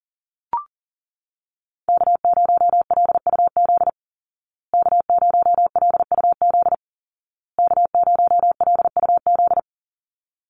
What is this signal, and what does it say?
Morse code, heard as a keyed tone, about 40 wpm, 700 Hz. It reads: K0LUZ K0LUZ K0LUZ
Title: Call Signs Repeated 2x - 40wpm